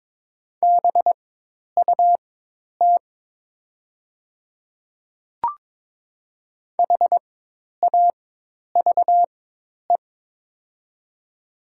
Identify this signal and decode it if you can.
Morse code, heard as a keyed tone, about 22 wpm, spaced as for 10 wpm, 700 Hz. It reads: BUT HAVE
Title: 4x Character Spacing - Top 100 Words - 22wpm